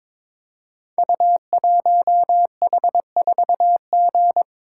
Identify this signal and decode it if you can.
Morse code, heard as a keyed tone, about 22 wpm, 700 Hz. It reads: U1H4G
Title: Code Groups - 5 Letter:Numbers - 22wpm